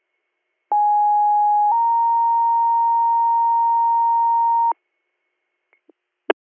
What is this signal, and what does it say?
Two-tone page, heard as a one-second tone, then a three-second tone, about 830 then 930 Hz. It is heard from 0.7 to 4.7 s.